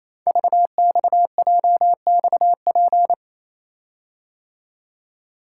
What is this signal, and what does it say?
Morse code, heard as a keyed tone, about 28 wpm, 700 Hz. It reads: VXJXP
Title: Code Groups - 5 Letter:Numbers - 28wpm